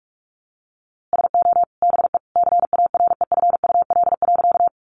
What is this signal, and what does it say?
Morse code, heard as a keyed tone, about 44 wpm, 700 Hz. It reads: SO BE CAREFUL.
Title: A Sound Of Thunder 45wpm